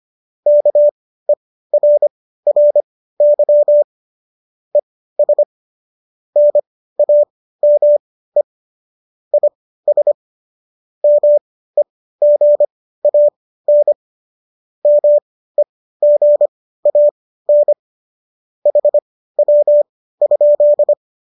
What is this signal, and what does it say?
Morse code, heard as a keyed tone, about 25 wpm, spaced as for 15 wpm, 590 Hz. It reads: KERRY ES NAME IS MEGAN MEGAN HW?